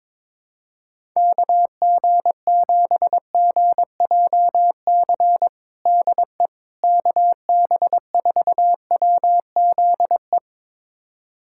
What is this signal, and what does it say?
Morse code, heard as a keyed tone, about 22 wpm, 700 Hz. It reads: KG7GJC DE KB4WZE
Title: USA Callsign DE Callsign 22wpm